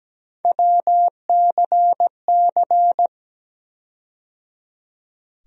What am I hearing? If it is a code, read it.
Morse code, heard as a keyed tone, about 17 wpm, 700 Hz. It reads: WCC
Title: USA Call Sign Postfixes - 3 Letters 17wpm